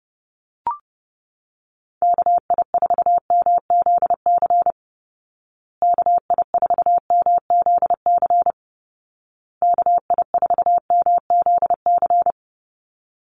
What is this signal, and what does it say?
Morse code, heard as a keyed tone, about 30 wpm, 700 Hz. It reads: KI4MZC KI4MZC KI4MZC
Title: Call Signs Repeated 2x - 30wpm